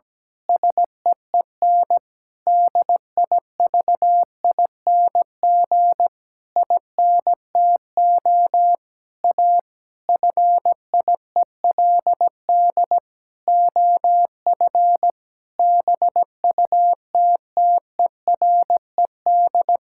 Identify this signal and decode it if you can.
Morse code, heard as a keyed tone, about 17 wpm, 700 Hz. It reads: SEEN DIVING INTO A FIELD OF BUTTERED
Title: Pangram Challenge v2 - 17wpm